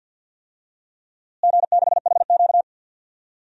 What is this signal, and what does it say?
Morse code, heard as a keyed tone, about 51 wpm, 700 Hz. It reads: G6HQ
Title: DXCC-100 Call Signs - 4 Characters - Repeated 3x - 50wpm